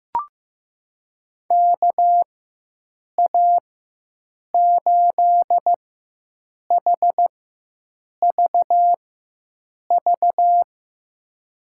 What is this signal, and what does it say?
Morse code, heard as a keyed tone, about 15 wpm, spaced as for 7 wpm, 700 Hz. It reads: KA8HVV
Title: USA Call Signs - 6 Characters - 4x Character Spacing - 15wpm